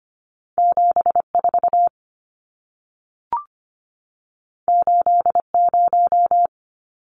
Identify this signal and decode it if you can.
Morse code, heard as a keyed tone, about 25 wpm, 700 Hz. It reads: 74 80
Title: Numbers - 2-Digit 25wpm